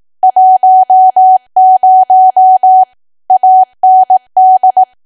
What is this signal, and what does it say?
Morse code, heard as a keyed tone, about 18 wpm, 750 Hz. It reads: 10 AND
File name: cwa / W1AW-Level2 / 18WPM